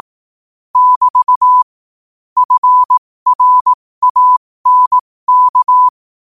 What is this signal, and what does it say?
Morse code, heard as a keyed tone, about 18 wpm, spaced as for 15 wpm, 1000 Hz. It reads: <BT> FRANK
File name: qotdfiles / 15wpm